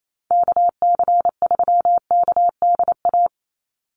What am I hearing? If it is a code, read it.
Morse code, heard as a keyed tone, about 28 wpm, 700 Hz. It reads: KC3KDA
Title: USA Call Signs 28wpm